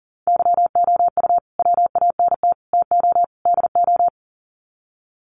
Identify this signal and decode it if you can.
Morse code, heard as a keyed tone, about 40 wpm, 700 Hz. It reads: YOU WANT TO DO
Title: Sets of 4 Words 40wpm